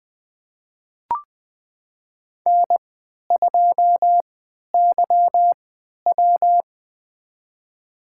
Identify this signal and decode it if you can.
Morse code, heard as a keyed tone, about 20 wpm, spaced as for 11 wpm, 700 Hz. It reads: N2YW
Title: USA Call Signs - 4 Characters - 3x Character Spacing - 20wpm